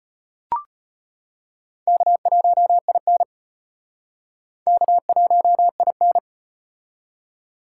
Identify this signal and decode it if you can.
Morse code, heard as a keyed tone, about 38 wpm, 700 Hz. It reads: K1IN K1IN